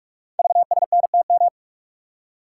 Morse code, heard as a keyed tone, about 44 wpm, 700 Hz. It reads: VSNTM
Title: Code Groups - 5 Letter:Numbers - 45wpm